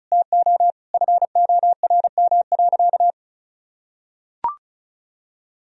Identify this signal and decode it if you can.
Morse code, heard as a keyed tone, about 34 wpm, 700 Hz. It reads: TO FORM.
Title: Sentences from Top 1000 Words - Review Part 1 - 35wpm